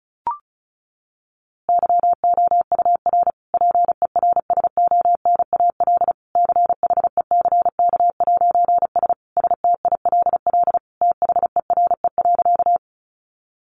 Morse code, heard as a keyed tone, about 34 wpm, 700 Hz. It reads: YOUR PERSONAL CHECK'S STILL THERE.
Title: A Sound Of Thunder 35wpm